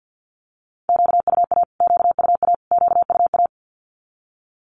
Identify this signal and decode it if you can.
Morse code, heard as a keyed tone, about 50 wpm, 700 Hz. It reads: QUA QUA QUA
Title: Q-Codes Repeated 3x 50wpm